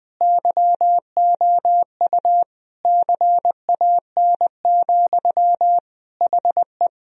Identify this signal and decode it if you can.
Morse code, heard as a keyed tone, about 21 wpm, 700 Hz. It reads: YOU CAN, HE
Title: Sentences from Top 1000 Words - Review Part 1 - 20wpm